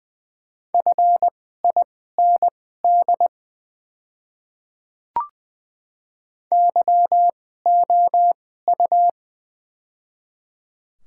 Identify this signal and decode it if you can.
Morse code, heard as a keyed tone, about 20 wpm, spaced as for 14 wpm, 700 Hz. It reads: FIND YOU
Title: 2x Character Spacing - Top 100 Words - 20wpm